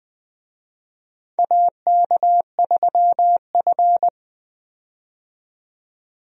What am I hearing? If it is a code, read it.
Morse code, heard as a keyed tone, about 20 wpm, 700 Hz. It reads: AK3F
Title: USA Call Signs - 4 Characters - 20wpm